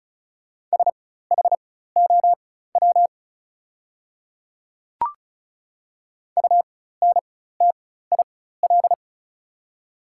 Morse code, heard as a keyed tone, about 36 wpm, spaced as for 16 wpm, 690 Hz. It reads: SHOW UNTIL
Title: 4x Character Spacing - Top 300 Words - 35wpm